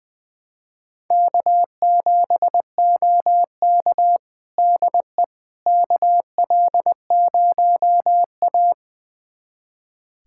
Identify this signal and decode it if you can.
Morse code, heard as a keyed tone, about 20 wpm, 700 Hz. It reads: K7OK DE KL0A
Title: USA Callsign DE Callsign - 4 Characters - 20wpm